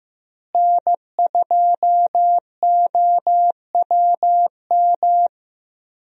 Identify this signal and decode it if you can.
Morse code, heard as a keyed tone, about 15 wpm, 700 Hz. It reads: N2OWM
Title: USA Call Signs - 5 Characters - Repeated 3x - 15wpm